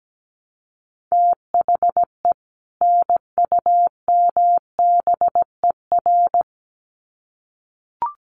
Morse code, heard as a keyed tone, about 17 wpm, 700 Hz. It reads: THE NUMBER
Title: Sets of 2 Words 17wpm